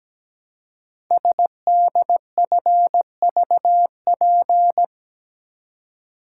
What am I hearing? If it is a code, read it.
Morse code, heard as a keyed tone, about 17 wpm, 700 Hz. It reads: SDFVP
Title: Code Groups - 5 Letter:Numbers - 17wpm